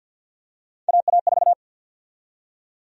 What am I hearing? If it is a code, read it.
Morse code, heard as a keyed tone, about 53 wpm, 700 Hz. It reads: AA4
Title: USA Call Sign Prefixes - 50wpm